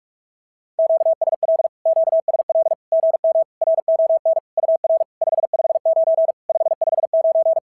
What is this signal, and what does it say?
Morse code, heard as a keyed tone, about 44 wpm, 650 Hz. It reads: QSL QSL GM RON UR 559 559
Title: SOTA QSO Practice 45wpm